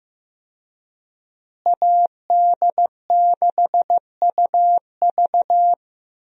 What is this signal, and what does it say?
Morse code, heard as a keyed tone, about 15 wpm, 700 Hz. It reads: AD6UV
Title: USA Call Signs - 5 Characters - Repeated 3x - 15wpm